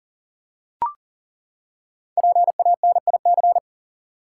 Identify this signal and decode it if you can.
Morse code, heard as a keyed tone, about 40 wpm, 700 Hz. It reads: PANIC
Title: Five Letter Words - 40wpm